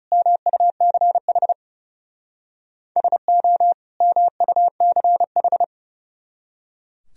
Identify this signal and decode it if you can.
Morse code, heard as a keyed tone, about 38 wpm, 700 Hz. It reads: MUCH SO MUCH